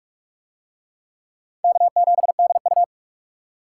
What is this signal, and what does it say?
Morse code, heard as a keyed tone, about 39 wpm, 690 Hz. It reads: K7DU